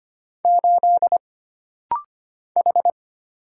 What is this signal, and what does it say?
Morse code, heard as a keyed tone, about 25 wpm, 700 Hz. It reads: 8 H
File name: Single Character (Letter:Number:Punctuation:Prosign) - Rapid-Fire - 25wpm